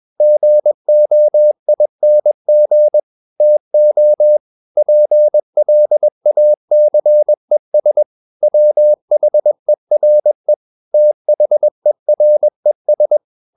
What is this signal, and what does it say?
Morse code, heard as a keyed tone, about 21 wpm, 600 Hz. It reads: GOING TO PLACES WHERE THERES
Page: CW med Gnister